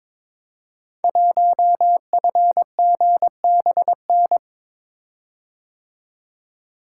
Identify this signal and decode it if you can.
Morse code, heard as a keyed tone, about 23 wpm, 700 Hz. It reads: 1FGBN
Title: Code Groups - 5 Letter:Numbers - 22wpm